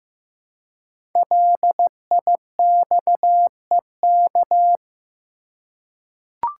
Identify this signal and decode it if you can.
Morse code, heard as a keyed tone, about 15 wpm, 700 Hz. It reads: LIXEK
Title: Code Groups - 5 Letter:Numbers - 15wpm